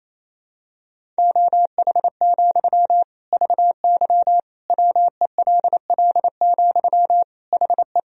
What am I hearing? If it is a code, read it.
Morse code, heard as a keyed tone, about 28 wpm, 700 Hz. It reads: OH, VY WELL, HE
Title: Sentences from Top 500 Words and Abbreviations 28wpm